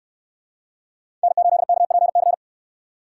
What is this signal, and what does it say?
Morse code, heard as a keyed tone, about 68 wpm, 700 Hz. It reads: N8KPC